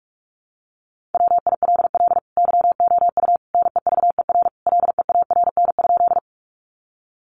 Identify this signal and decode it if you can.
Morse code, heard as a keyed tone, about 44 wpm, 700 Hz. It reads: WILL YOU NEVER LEARN?
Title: Sentences from Top 300 Words 45wpm